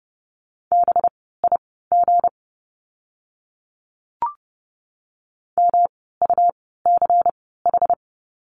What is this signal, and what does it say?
Morse code, heard as a keyed tone, about 30 wpm, spaced as for 17 wpm, 700 Hz. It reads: BIG MUCH
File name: course / 30:10wpm / 237 Lesson 38f - Letter q - Words 30wpm